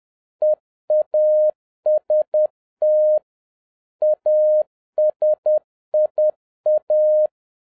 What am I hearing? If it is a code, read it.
Morse code, heard as a keyed tone, about 10 wpm, 610 Hz. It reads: EAST ASIA
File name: 2011-07-mp3 / n36 THE 3.11 2 10wpm